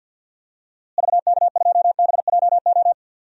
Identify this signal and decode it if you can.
Morse code, heard as a keyed tone, about 48 wpm, 700 Hz. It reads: VK2BJO
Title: DXCC-100 Call Signs - 6 Characters - Repeated 2x - 50wpm